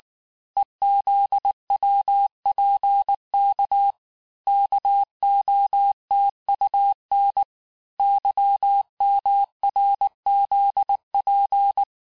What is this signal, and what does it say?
Morse code, heard as a keyed tone, about 19 wpm, 800 Hz. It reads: EZWPK KOTUN YMRZP
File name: cours-du-lundi files / cours-1583--texte-chiffre-18-mpm